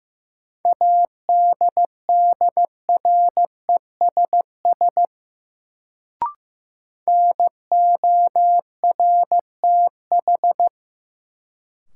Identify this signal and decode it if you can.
Morse code, heard as a keyed tone, about 15 wpm, 700 Hz. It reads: ADDRESS NORTH
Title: Sets of 5 Words Encore - New Word Drill - 15wpm